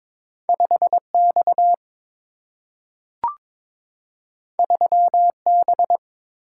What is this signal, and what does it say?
Morse code, heard as a keyed tone, about 22 wpm, 700 Hz. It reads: 5 X 3 B
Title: Single-Letter Single-Number 22wpm